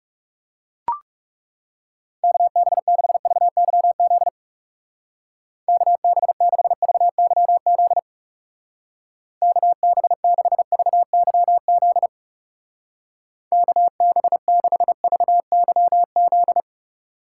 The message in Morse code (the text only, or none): KB6VYZ KB6VYZ KB6VYZ KB6VYZ